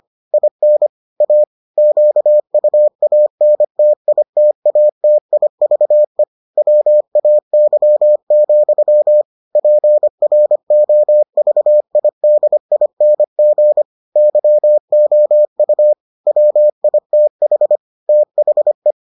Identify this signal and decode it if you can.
Morse code, heard as a keyed tone, about 25 wpm, 600 Hz. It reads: IN A QUANTITATIVE WAY, PROVIDING YOU WITH THE